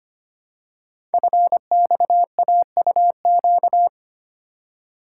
Morse code, heard as a keyed tone, about 25 wpm, 700 Hz. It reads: FXAUQ
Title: Code Groups - 5 Letters - 25wpm